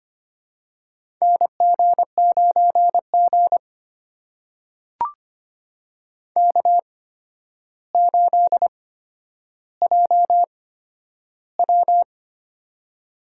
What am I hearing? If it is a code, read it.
Morse code, heard as a keyed tone, about 26 wpm, spaced as for 11 wpm, 700 Hz. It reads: NG9G K8JW
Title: USA Call Signs - 4 Characters - 8x Character Spacing - 25wpm